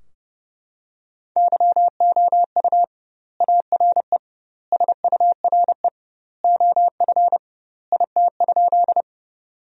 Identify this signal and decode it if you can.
Morse code, heard as a keyed tone, about 30 wpm, 700 Hz. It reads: YOU ARE SURE OF IT?
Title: 2x Word-Spacing - Sentences from Top 300 Words - 30wpm